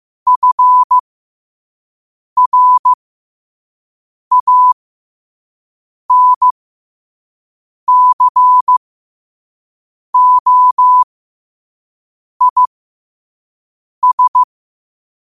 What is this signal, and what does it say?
Morse code, heard as a keyed tone, about 15 wpm, spaced as for 5 wpm, 1000 Hz. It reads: FRANCOIS
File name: qotdfiles / 5wpm